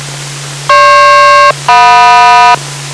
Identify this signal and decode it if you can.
Aircraft SELCAL, heard as two pairs of tones. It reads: GP-JM